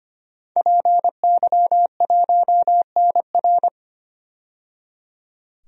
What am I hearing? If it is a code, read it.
Morse code, heard as a keyed tone, about 25 wpm, 700 Hz. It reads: PY1NR
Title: DXCC-100 Call Signs - 5 Characters - 25wpm